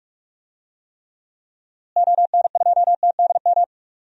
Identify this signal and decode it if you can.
Morse code, heard as a keyed tone, about 46 wpm, 700 Hz. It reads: ON2TDM
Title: DXCC-100 Call Signs - 6 Characters - Repeated 2x - 45wpm